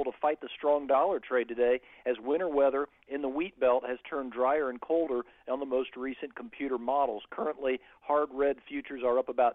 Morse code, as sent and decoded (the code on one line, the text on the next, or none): none